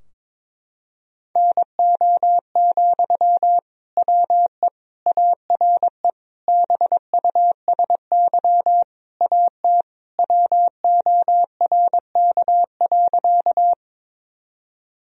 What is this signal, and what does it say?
Morse code, heard as a keyed tone, about 22 wpm, 700 Hz. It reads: NO, WE ARE BUSY AT WORK.
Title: Sentences from Top 1000 Words - Review Part 3 - 22wpm